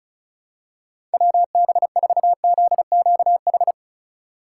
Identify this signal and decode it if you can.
Morse code, heard as a keyed tone, about 36 wpm, 700 Hz. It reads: WB4ZQH
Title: USA Call Signs - 6 Characters - 2x Character Spacing - 35wpm